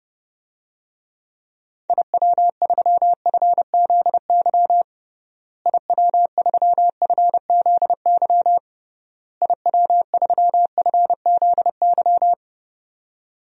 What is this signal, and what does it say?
Morse code, heard as a keyed tone, about 30 wpm, 700 Hz. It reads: IW3FZY IW3FZY IW3FZY
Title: DXCC-100 Call Signs - 6 Characters - Repeated 3x - 30wpm